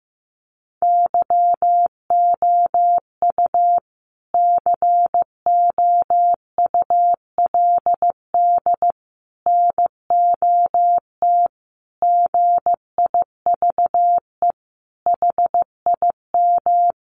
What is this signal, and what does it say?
Morse code, heard as a keyed tone, about 15 wpm, 700 Hz. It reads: YOU COULD NOT GIVE HIM
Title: Sentences from Top 1500 Words 15wpm